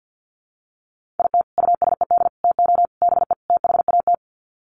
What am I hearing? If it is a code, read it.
Morse code, heard as a keyed tone, about 50 wpm, 700 Hz. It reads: IT USED TO BE THAT
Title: Sets of 5 Words Encore2 50wpm